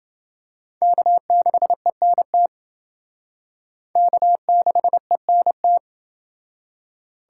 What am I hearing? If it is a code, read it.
Morse code, heard as a keyed tone, about 28 wpm, 700 Hz. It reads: K6ENT K6ENT